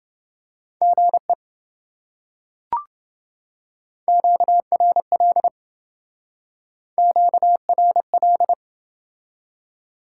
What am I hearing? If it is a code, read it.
Morse code, heard as a keyed tone, about 30 wpm, 700 Hz. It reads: GE QRL QRL